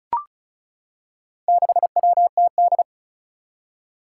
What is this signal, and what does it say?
Morse code, heard as a keyed tone, about 36 wpm, 700 Hz. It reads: 6WTD
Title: Four Challenging Letter-Number Combinations 35wpm